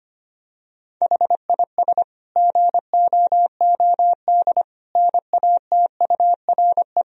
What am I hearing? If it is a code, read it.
Morse code, heard as a keyed tone, about 25 wpm, 700 Hz. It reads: HIS GOOD NATURE
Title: Sentences from Top 1000 Words - Review Part 1 - 25wpm